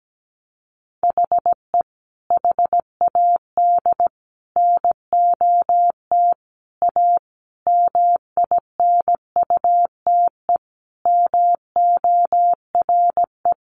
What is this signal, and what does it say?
Morse code, heard as a keyed tone, about 17 wpm, 700 Hz. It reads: HE HAD NOT A MINUTE MORE
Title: Sentences from Top 1100 Words 17wpm